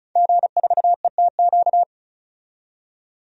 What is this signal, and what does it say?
Morse code, heard as a keyed tone, about 36 wpm, 700 Hz. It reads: G4ETQ
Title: DXCC-100 Call Signs - 5 Characters - 35wpm